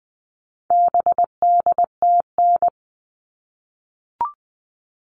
Morse code, heard as a keyed tone, about 20 wpm, 700 Hz. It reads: BDTN
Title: All Cut Numbers - 4-Digit 20wpm